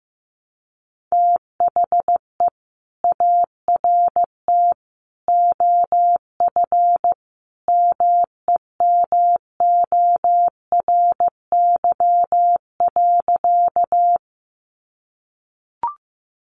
Morse code, heard as a keyed tone, about 15 wpm, 700 Hz. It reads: THE ART OF MEMORY.
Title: Sentences from Top 1300 Words 15wpm